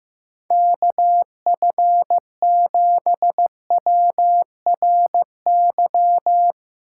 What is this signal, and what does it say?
Morse code, heard as a keyed tone, about 15 wpm, 700 Hz. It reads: KF7WRY
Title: USA Call Signs - 6 Characters - 3x Character Spacing - 15wpm